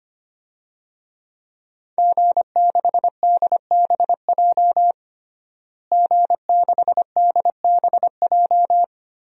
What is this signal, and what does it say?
Morse code, heard as a keyed tone, about 25 wpm, 700 Hz. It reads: G6DBJ G6DBJ
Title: DXCC-100 Call Signs - 5 Characters - Repeated 3x - 25wpm